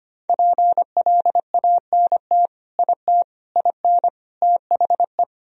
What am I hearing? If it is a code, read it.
Morse code, heard as a keyed tone, about 25 wpm, 700 Hz. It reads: PLANT IT IN THE
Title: Sentences from Top 1000 Words - Review Part 4 - 25wpm